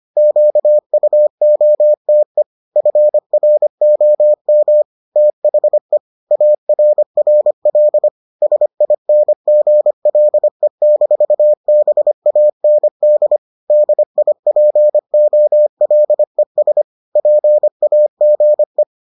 Morse code, heard as a keyed tone, about 25 wpm, 600 Hz. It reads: QUOTE FROM THE ARRL SINGLE-BAND DIPOLES PAGE